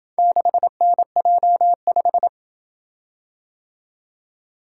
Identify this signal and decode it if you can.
Morse code, heard as a keyed tone, about 23 wpm, 700 Hz. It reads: 6NJ5
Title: Speed-Racing - Four Letter-Number Combinations - 20wpm